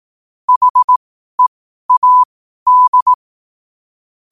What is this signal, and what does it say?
Morse code, heard as a keyed tone, about 18 wpm, spaced as for 12 wpm, 1000 Hz. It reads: HEAD
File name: qotdfiles / 12wpm